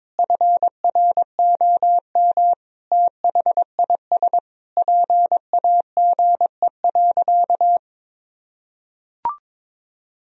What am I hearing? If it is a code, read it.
Morse code, heard as a keyed tone, about 22 wpm, 700 Hz. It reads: FROM THIS PAGE.
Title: Sentences from Top 1000 Words - Review Part 1 - 22wpm